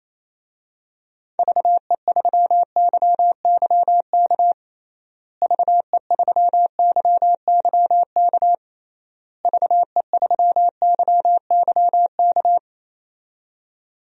Morse code, heard as a keyed tone, about 28 wpm, 700 Hz. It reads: VE3YYK VE3YYK VE3YYK
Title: DXCC-100 Call Signs - 6 Characters - Repeated 3x - 28wpm